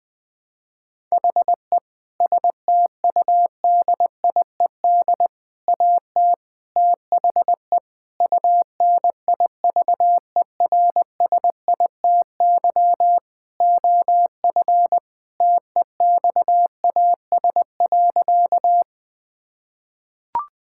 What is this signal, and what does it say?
Morse code, heard as a keyed tone, about 20 wpm, 700 Hz. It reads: HE STUDIED AT THE UNIVERSITY OF TEXAS.
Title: Sentences with 4-Word Phrases - Spoken Beginning - 20wpm